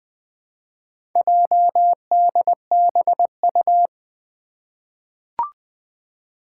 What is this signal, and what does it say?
Morse code, heard as a keyed tone, about 20 wpm, 700 Hz. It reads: JDBU
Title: Four Challenging Letter-Number Combinations 20wpm